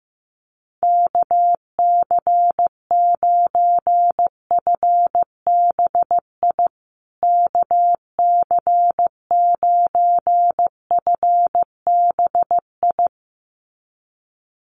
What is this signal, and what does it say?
Morse code, heard as a keyed tone, about 15 wpm, 700 Hz. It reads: KC9FBI KC9FBI
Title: USA Call Signs Repeated 2x 15wpm